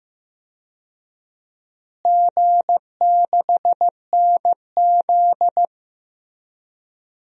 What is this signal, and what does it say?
Morse code, heard as a keyed tone, about 15 wpm, 700 Hz. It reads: G6NZ
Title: DXCC-100 Call Signs - 4 Characters - Repeated 2x - 15wpm